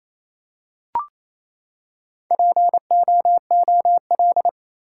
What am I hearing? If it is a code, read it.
Morse code, heard as a keyed tone, about 28 wpm, 700 Hz. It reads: POOL
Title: Four Letter Words - 28wpm